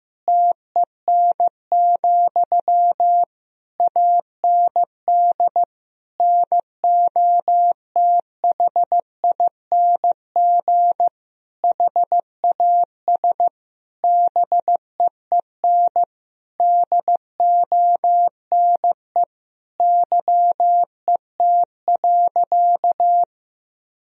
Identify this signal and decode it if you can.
Morse code, heard as a keyed tone, about 15 wpm, 700 Hz. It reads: TEN, AND NOTHING HAS BEEN DONE YET.
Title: Sentences from Top 1000 Words - Review Part 2 - 15wpm